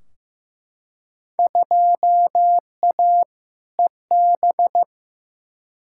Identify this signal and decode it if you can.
Morse code, heard as a keyed tone, about 15 wpm, 700 Hz. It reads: 2A EB
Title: Field Day Exchanges - 15wpm